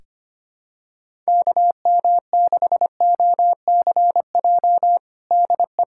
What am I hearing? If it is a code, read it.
Morse code, heard as a keyed tone, about 25 wpm, 700 Hz. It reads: KM6OCJ DE
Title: USA Callsign DE Callsign - 6 Characters - 25wpm